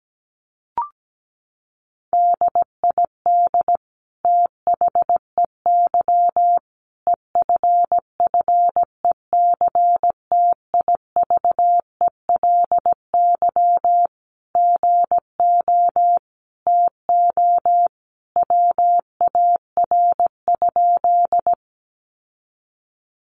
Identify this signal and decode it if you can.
Morse code, heard as a keyed tone, about 17 wpm, 700 Hz. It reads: DID THEY EFFECTIVELY GO TO WAR?
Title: Sentences from Top 1400 Words 17wpm